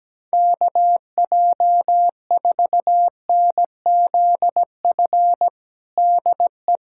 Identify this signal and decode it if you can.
Morse code, heard as a keyed tone, about 17 wpm, 700 Hz. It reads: KJ4NZF DE
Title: USA Callsign DE Callsign - 6 Characters - 17wpm